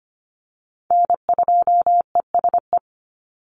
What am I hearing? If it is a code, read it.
Morse code, heard as a keyed tone, about 25 wpm, 700 Hz. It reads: N2ESE
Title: NA SOTA Chaser Callsigns 25wpm